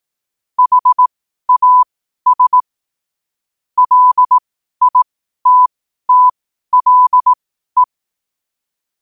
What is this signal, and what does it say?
Morse code, heard as a keyed tone, about 18 wpm, spaced as for 12 wpm, 1000 Hz. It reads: HAS LITTLE
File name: qotdfiles / 12wpm